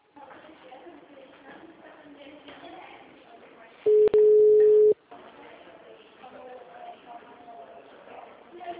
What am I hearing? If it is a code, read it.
Ringback tone (Europe, 425 Hz), with speech.
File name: out- -2002- - - .57794